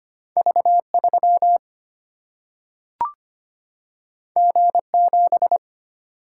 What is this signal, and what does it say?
Morse code, heard as a keyed tone, about 25 wpm, 700 Hz. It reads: V 3 G 7
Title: Single-Letter Single-Number 25wpm